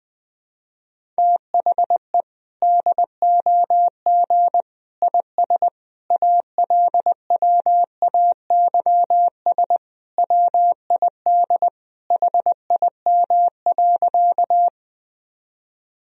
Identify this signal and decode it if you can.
Morse code, heard as a keyed tone, about 21 wpm, 700 Hz. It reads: THE DOG IS ALWAYS WID HIM.
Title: Sentences from Top 400 Words and Abbreviations 20wpm